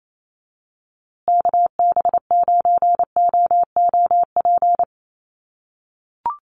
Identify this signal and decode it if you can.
Morse code, heard as a keyed tone, about 28 wpm, 700 Hz. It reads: KB9OOP
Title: USA Call Signs Repeated 3x 28wpm